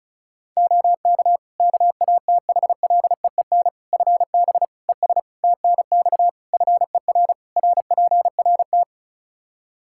Morse code, heard as a keyed tone, about 36 wpm, 700 Hz. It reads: OK KATHLEEN FB ES TNX FER RPRT
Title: Practice QSO - Exchange 2 - Info - 35wpm